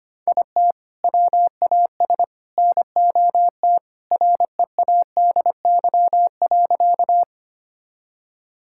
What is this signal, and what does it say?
Morse code, heard as a keyed tone, about 25 wpm, 700 Hz. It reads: IT WAS NOT READY.
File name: practice / 25wpm / Sentences with 2-Word Phrases - 25wpm